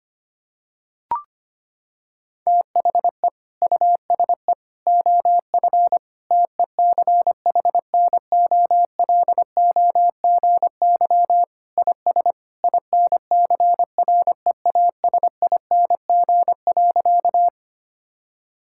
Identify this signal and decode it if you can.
Morse code, heard as a keyed tone, about 25 wpm, 700 Hz. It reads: THE USE OF TECHNOLOGY IS INCREASING.
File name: practice / 25wpm / Sentences with 2-Word Phrases - 25wpm